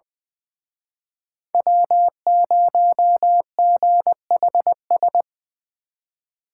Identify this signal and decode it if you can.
Morse code, heard as a keyed tone, about 24 wpm, 700 Hz. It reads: W0GHS
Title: Speed Racing - USA Call Signs 20wpm